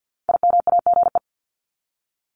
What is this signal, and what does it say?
Morse code, heard as a keyed tone, about 50 wpm, 700 Hz. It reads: IMAGE
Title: Top 501-750 Words 50wpm